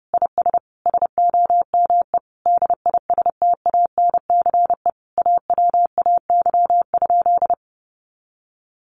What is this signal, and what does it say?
Morse code, heard as a keyed tone, about 30 wpm, 700 Hz. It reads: IS SOME DISTANCE AWAY?
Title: Sentences from Top 1200 Words 30wpm